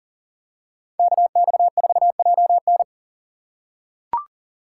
Morse code, heard as a keyed tone, about 38 wpm, 700 Hz. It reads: KX4JN
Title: USA Call Signs - 5 Characters - 4x Character Spacing - 40wpm